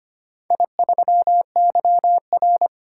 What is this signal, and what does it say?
Morse code, heard as a keyed tone, about 25 wpm, 700 Hz. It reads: I3YR
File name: practice / 25wpm / Four Letter-Number Combinations - 25wpm